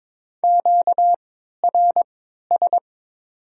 Morse code, heard as a keyed tone, about 22 wpm, spaced as for 13 wpm, 700 Hz. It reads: QRS
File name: Top 90 QSO Elements - 3x Character Spacing - 22wpm